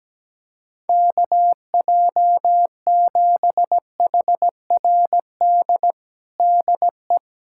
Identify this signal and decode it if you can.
Morse code, heard as a keyed tone, about 17 wpm, 700 Hz. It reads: KJ7HRD DE
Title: USA Callsign DE Callsign 17wpm